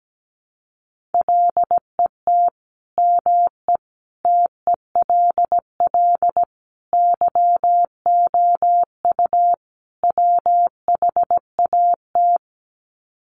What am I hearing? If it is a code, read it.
Morse code, heard as a keyed tone, about 17 wpm, 700 Hz. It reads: LET ME TELL YOU WHAT
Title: Sets of 5 Words Encore 17wpm